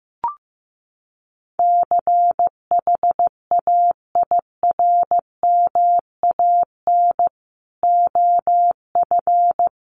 Morse code, heard as a keyed tone, about 15 wpm, 700 Hz. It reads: CHAIRMAN OF
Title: Sets of 3 Words 15wpm